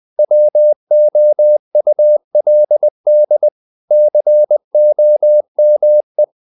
Morse code, heard as a keyed tone, about 20 wpm, 600 Hz. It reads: WOULD COME
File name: the old man and the sea / the old man and the sea part 1 20wpm